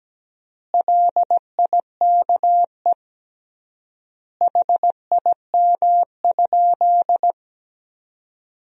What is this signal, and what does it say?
Morse code, heard as a keyed tone, about 17 wpm, 700 Hz. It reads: LIKE HIM?
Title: 3x Word-Spacing - Sentences from Top 100 Words - 17wpm